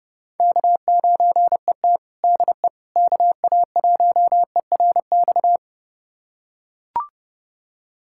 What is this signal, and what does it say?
Morse code, heard as a keyed tone, about 30 wpm, 700 Hz. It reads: K9ET DE KA1ERX
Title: USA Callsign DE Callsign 30wpm